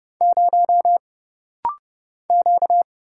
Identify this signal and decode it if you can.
Morse code, heard as a keyed tone, about 30 wpm, 700 Hz. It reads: 0 Q
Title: Single Letter-Number - Rapid-Fire 30wpm